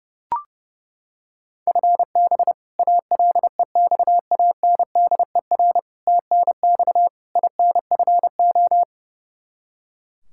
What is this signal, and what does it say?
Morse code, heard as a keyed tone, about 32 wpm, 700 Hz. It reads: FB ALEXANDER TNX INFO
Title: K1USN SST QSO Practice 30wpm